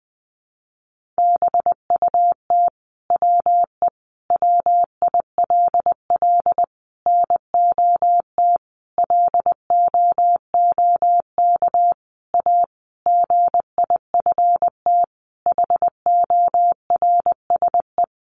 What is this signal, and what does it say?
Morse code, heard as a keyed tone, about 20 wpm, 700 Hz. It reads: BUT WE WILL NOT LOOK A GIFT HORSE
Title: Sentences from Top 1500 Words 20wpm